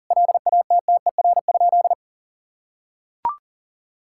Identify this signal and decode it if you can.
Morse code, heard as a keyed tone, about 40 wpm, 700 Hz. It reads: LATTER?
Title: Sentences from Top 1600 Words 40wpm